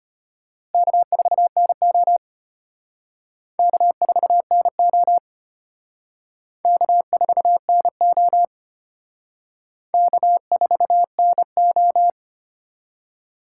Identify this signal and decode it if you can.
Morse code, heard as a keyed tone, about 34 wpm, 700 Hz. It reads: K4NO K4NO K4NO K4NO